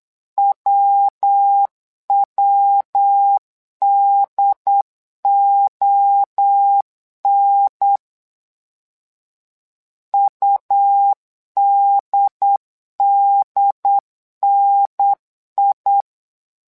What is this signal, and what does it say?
Morse code, heard as a keyed tone, about 8 wpm, 800 Hz. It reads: WWDON UDDNI
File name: LEZIONI / 3 MWRO / 3 MWRO 8wpm 2606